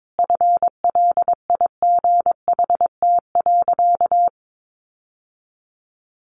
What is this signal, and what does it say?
Morse code, heard as a keyed tone, about 22 wpm, 700 Hz. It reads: FLIGHT.
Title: Sentences from Top 1400 Words 22wpm